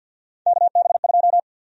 Morse code, heard as a keyed tone, about 51 wpm, 700 Hz. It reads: KB2
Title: USA Call Sign Prefixes - Repeated 2x - 50wpm